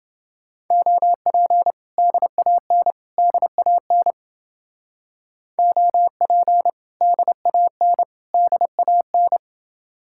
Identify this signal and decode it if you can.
Morse code, heard as a keyed tone, about 30 wpm, 700 Hz. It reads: OP DAN DAN OP DAN DAN